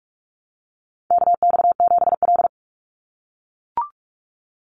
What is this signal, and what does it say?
Morse code, heard as a keyed tone, about 44 wpm, 700 Hz. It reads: KX7L
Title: NA SOTA Chaser Callsigns 45wpm